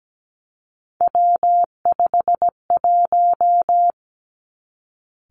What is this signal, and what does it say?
Morse code, heard as a keyed tone, about 17 wpm, 700 Hz. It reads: W51
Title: Three Challenging Letter-Number Combinations 17wpm